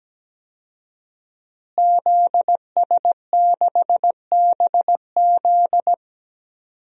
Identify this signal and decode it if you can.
Morse code, heard as a keyed tone, about 17 wpm, 700 Hz. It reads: ZS6BZ
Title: DXCC-100 Call Signs - 5 Characters - Repeated 2x - 17wpm